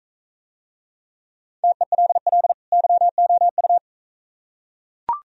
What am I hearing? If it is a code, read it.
Morse code, heard as a keyed tone, about 40 wpm, 700 Hz. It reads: TELL YOU